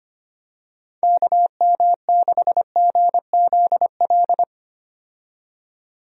Morse code, heard as a keyed tone, about 25 wpm, 700 Hz. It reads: KM6GZL
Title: USA Call Signs - 6 Characters - 25wpm